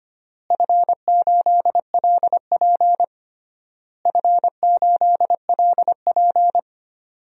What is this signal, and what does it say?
Morse code, heard as a keyed tone, about 25 wpm, 700 Hz. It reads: F8LP F8LP
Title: DXCC-100 Call Signs - 4 Characters - Repeated 3x - 25wpm